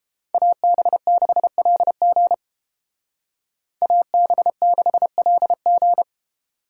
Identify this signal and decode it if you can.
Morse code, heard as a keyed tone, about 31 wpm, 700 Hz. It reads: AB6LG AB6LG